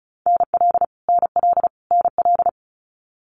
Morse code, heard as a keyed tone, about 36 wpm, 700 Hz. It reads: NL NL NL
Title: Canadian Provinces and Territories - Repeated 3x - 35wpm